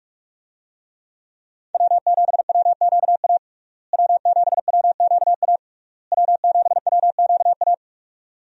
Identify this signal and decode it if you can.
Morse code, heard as a keyed tone, about 46 wpm, 700 Hz. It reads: W7WQA W7WQA W7WQA
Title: USA Call Signs - 5 Characters - Repeated 3x - 45wpm